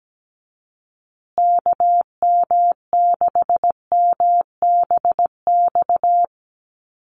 Morse code, heard as a keyed tone, about 17 wpm, 700 Hz. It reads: KM6MBX
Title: USA Call Signs Repeated 3x 17wpm